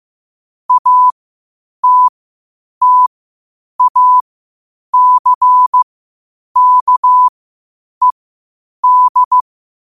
Morse code, heard as a keyed tone, about 15 wpm, spaced as for 8 wpm, 1000 Hz. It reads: ATTACKED
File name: qotdfiles / 8wpm